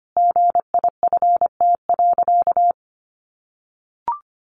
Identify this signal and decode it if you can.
Morse code, heard as a keyed tone, about 25 wpm, 700 Hz. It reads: GIFT.
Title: Easy Phrases 25wpm